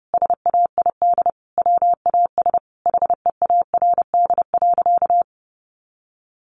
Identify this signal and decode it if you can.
Morse code, heard as a keyed tone, about 30 wpm, 700 Hz. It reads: SAID WAS HEARD.
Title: Sentences from Top 400 Words 30wpm